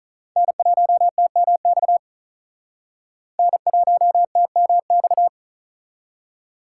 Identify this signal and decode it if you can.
Morse code, heard as a keyed tone, about 44 wpm, 700 Hz. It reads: N1TMX N1TMX